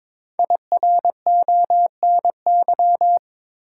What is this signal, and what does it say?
Morse code, heard as a keyed tone, about 22 wpm, 700 Hz. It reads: IRONY
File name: 5 Letter Words - First 3 Letters Given - 22wpm